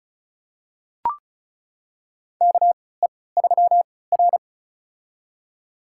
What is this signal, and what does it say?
Morse code, heard as a keyed tone, about 36 wpm, spaced as for 20 wpm, 700 Hz. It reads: KE3R
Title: USA Call Signs - 4 Characters - 3x Character Spacing - 35wpm